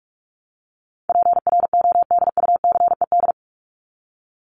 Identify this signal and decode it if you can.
Morse code, heard as a keyed tone, about 45 wpm, 700 Hz. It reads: PRODUCED
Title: ED Suffix - Top 100 Words with Base Words 45wpm